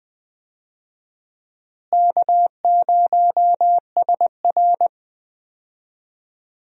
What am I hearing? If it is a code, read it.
Morse code, heard as a keyed tone, about 20 wpm, 700 Hz. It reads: K0SR
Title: USA Call Signs - 4 Characters - Repeated 2x - 20wpm